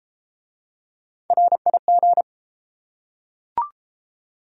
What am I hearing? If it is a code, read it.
Morse code, heard as a keyed tone, about 31 wpm, 700 Hz. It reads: RIG